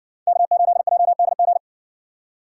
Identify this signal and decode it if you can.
Morse code, heard as a keyed tone, about 56 wpm, 700 Hz. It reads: K8JDG